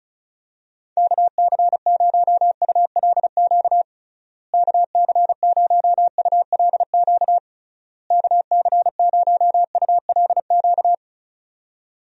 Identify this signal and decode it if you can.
Morse code, heard as a keyed tone, about 34 wpm, 700 Hz. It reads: KC0ULQ KC0ULQ KC0ULQ
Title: USA Call Signs - 6 Characters - Repeated 3x - 35wpm